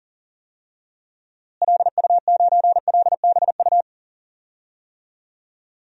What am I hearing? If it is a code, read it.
Morse code, heard as a keyed tone, about 40 wpm, 700 Hz. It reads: LU9LBU
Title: DXCC-100 Call Signs - 6 Characters - Repeated 2x - 40wpm